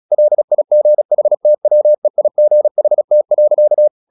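Morse code, heard as a keyed tone, about 36 wpm, 600 Hz. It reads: LIGHTWEIGHT.
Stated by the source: Morse practice at 36 wpm, text of Ken Follett's novel Never, from CW med Gnister.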